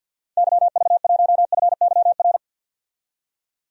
Morse code, heard as a keyed tone, about 52 wpm, 700 Hz. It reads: YV1FYR
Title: DXCC-100 Call Signs - 6 Characters - 50wpm